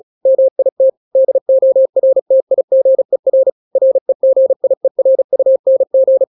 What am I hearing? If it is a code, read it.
Morse code, heard as a keyed tone, about 36 wpm, 530 Hz. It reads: MIT DORTIGER REGIERUNG
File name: daily / 35wpm